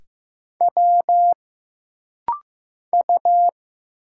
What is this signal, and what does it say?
Morse code, heard as a keyed tone, about 15 wpm, 700 Hz. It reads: W U
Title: Single Letters - Rapid-Fire 15wpm